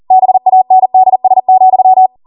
Morse code, heard as a keyed tone, about 40 wpm, 750 Hz. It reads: BANDS,
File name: Morse / 40WPM